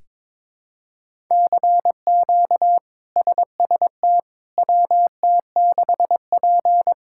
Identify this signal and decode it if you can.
Morse code, heard as a keyed tone, about 22 wpm, 700 Hz. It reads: CQ SST WT6P
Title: K1USN SST QSO Practice - 22wpm